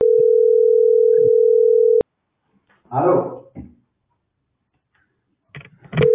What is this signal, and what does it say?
Ringback tone (North America), 440 + 480 Hz.